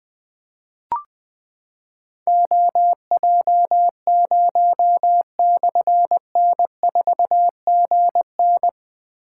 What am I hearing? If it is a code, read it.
Morse code, heard as a keyed tone, about 20 wpm, 700 Hz. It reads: OJ0/N4GN
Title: Difficult Call Signs 20wpm